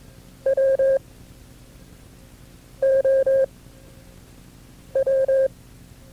Morse code, heard as a keyed tone, about 22 wpm, 540 Hz. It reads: W O W
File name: FreeFiles / CW-practice / KB6MT-tape2-side1